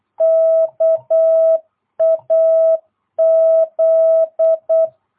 Morse code, heard as a keyed tone, about 8 wpm, 660 Hz. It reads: KAZ